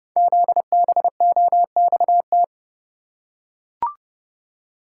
Morse code, heard as a keyed tone, about 30 wpm, 700 Hz. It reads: ZBOXT
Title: Code Groups - 5 Letters - 30wpm